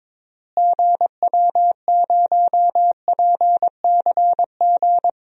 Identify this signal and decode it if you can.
Morse code, heard as a keyed tone, about 22 wpm, 700 Hz. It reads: GW0PCG
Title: DXCC-100 Call Signs - 6 Characters - Repeated 3x - 22wpm